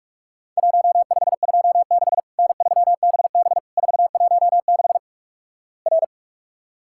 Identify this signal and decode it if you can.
Morse code, heard as a keyed tone, about 44 wpm, 700 Hz. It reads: 1526 N3BB 416 R
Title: Worked All Europe DX Contest - QTC Exchange - 45wpm